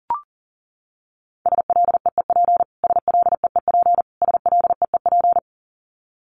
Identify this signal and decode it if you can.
Morse code, heard as a keyed tone, about 40 wpm, 700 Hz. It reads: SLEEP SLEEP SLEEP
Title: Top 500 Words Repeated 40wpm